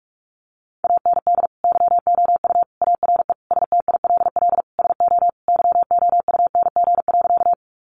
Morse code, heard as a keyed tone, about 44 wpm, 700 Hz. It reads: AND YOU ARE STILL SO YOUNG.
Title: Sentences from Top 400 Words 45wpm